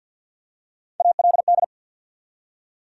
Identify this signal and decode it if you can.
Morse code, heard as a keyed tone, about 50 wpm, 700 Hz. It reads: ALD
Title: USA Call Sign Postfixes - 3 Letters 50wpm